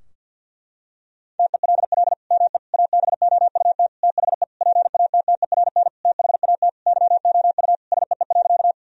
Morse code, heard as a keyed tone, about 47 wpm, 700 Hz. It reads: TELL ME ABOUT THE PATTERN THAT YOU SEE.
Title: Sentences from Top 1000 Words - Review Part 2 - 50wpm